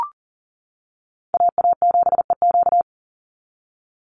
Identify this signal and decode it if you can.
Morse code, heard as a keyed tone, about 40 wpm, 700 Hz. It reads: AA7EQ
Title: NA SOTA Chaser Callsigns 40wpm